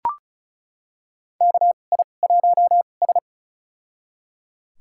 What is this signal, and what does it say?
Morse code, heard as a keyed tone, about 36 wpm, spaced as for 26 wpm, 690 Hz. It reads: KI1S
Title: USA Call Signs - 4 Characters - 2x Character Spacing - 35wpm